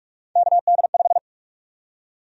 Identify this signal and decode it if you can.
Morse code, heard as a keyed tone, about 46 wpm, 700 Hz. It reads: KD5
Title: USA Call Sign Prefixes - Repeated 2x - 45wpm